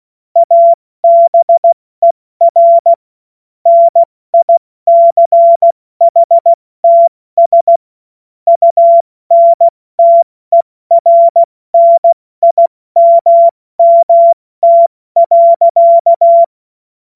Morse code, heard as a keyed tone, about 16 wpm, spaced as for 14 wpm, 680 Hz. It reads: ABER NICHTS UNTERNIMMT.
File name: Morsekurs / Witze / Witz 13. Dezember 2022 70 F80